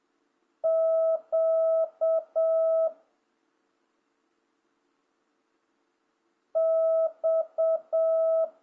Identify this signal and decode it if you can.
Morse code, heard as a keyed tone, about 7 wpm, 1300 Hz. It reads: Q X